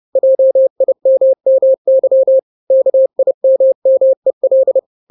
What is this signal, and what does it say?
Morse code, heard as a keyed tone, about 30 wpm, 530 Hz. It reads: JIMMY KIMMEL
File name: 29wpm